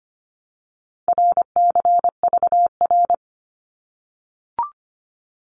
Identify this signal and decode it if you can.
Morse code, heard as a keyed tone, about 25 wpm, 700 Hz. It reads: RCVR
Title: Abbreviations 25wpm